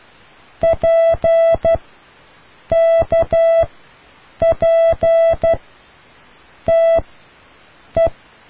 Morse code, heard as a keyed tone, about 12 wpm, spaced as for 6 wpm, 660 Hz. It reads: PKPTE